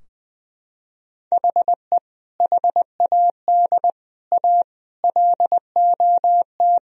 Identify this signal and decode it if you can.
Morse code, heard as a keyed tone, about 21 wpm, 700 Hz. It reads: HE HAD A LOT
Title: Sentences from Top 1000 Words - Review Part 3 - 20wpm